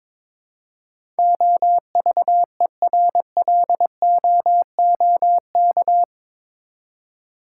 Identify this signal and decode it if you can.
Morse code, heard as a keyed tone, about 22 wpm, 700 Hz. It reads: OVERLOOK
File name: practice / 22wpm / Eight Letter Words - 22wpm